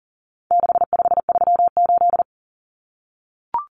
Morse code, heard as a keyed tone, about 40 wpm, 700 Hz. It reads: 6538
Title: Common Cut Numbers - 4-Digit 40wpm